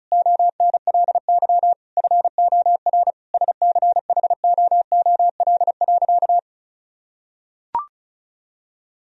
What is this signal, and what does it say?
Morse code, heard as a keyed tone, about 34 wpm, 700 Hz. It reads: ONLY FOR SCHOOL.
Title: Sentences from Top 1000 Words - Review Part 1 - 35wpm